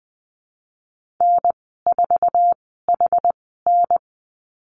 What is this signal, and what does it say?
Morse code, heard as a keyed tone, about 20 wpm, spaced as for 14 wpm, 700 Hz. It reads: N4HN
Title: 069 Lesson 12g - Letter d - Call Signs 20wpm